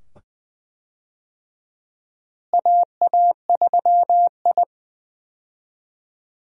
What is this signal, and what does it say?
Morse code, heard as a keyed tone, about 20 wpm, 700 Hz. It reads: AA3I
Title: USA Call Signs - 4 Characters - Repeated 2x - 20wpm